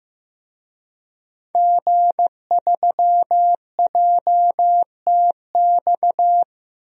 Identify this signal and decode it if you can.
Morse code, heard as a keyed tone, about 15 wpm, 700 Hz. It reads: G3JTX
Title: DXCC-100 Call Signs - 5 Characters - Repeated 3x - 15wpm